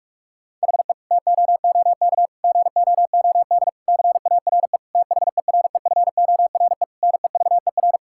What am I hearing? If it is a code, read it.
Morse code, heard as a keyed tone, about 45 wpm, 690 Hz. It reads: HE TOOK GOOD CARE THEREFORE NEVER
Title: Sentences from Top 1000 Words - Review Part 5 - 45wpm